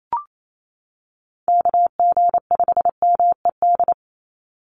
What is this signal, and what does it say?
Morse code, heard as a keyed tone, about 28 wpm, 700 Hz. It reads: KG5MED
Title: USA Call Signs 28wpm